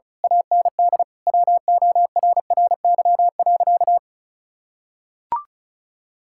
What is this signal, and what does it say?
Morse code, heard as a keyed tone, about 36 wpm, 700 Hz. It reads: AND WORRY.
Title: Pangram Challenge v2 - 35wpm